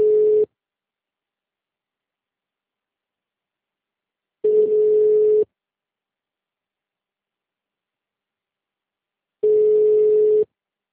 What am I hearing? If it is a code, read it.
Ringback tone (Europe, 425 Hz).